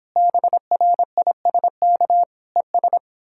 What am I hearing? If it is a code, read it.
Morse code, heard as a keyed tone, about 31 wpm, 700 Hz. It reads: BRISK ES